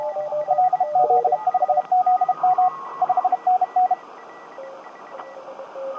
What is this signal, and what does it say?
Morse code, heard as a keyed tone, about 32 wpm, 750 Hz. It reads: LZ5ZM 5NN
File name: tt8tt-10m-cw-here-in-jn90ab